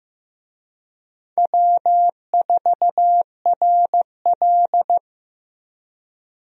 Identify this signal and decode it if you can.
Morse code, heard as a keyed tone, about 15 wpm, 700 Hz. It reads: W4RL
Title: USA Call Signs - 4 Characters - Repeated 2x - 15wpm